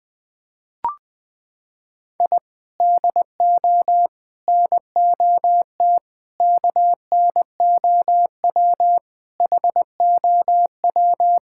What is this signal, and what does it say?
Morse code, heard as a keyed tone, about 19 wpm, 700 Hz. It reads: I DO NOT KNOW HOW
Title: Sentences with 5-Word Phrases - 20wpm